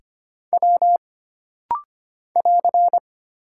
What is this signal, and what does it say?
Morse code, heard as a keyed tone, about 25 wpm, 700 Hz. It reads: W <AR>
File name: Single Character (Letter:Number:Punctuation:Prosign) - Rapid-Fire - 25wpm